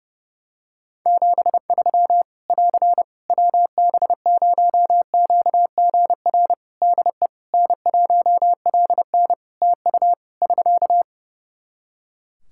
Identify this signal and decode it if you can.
Morse code, heard as a keyed tone, about 28 wpm, 700 Hz. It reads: 73 <AR> WB0QGR DE N1LN TU <SK>
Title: Practice QSO - Exchange 4 - Ending - 30wpm